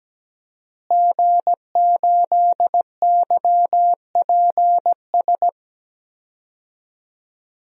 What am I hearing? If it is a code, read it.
Morse code, heard as a keyed tone, about 17 wpm, 700 Hz. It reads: G8YPS
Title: DXCC-100 Call Signs - 5 Characters - Repeated 2x - 17wpm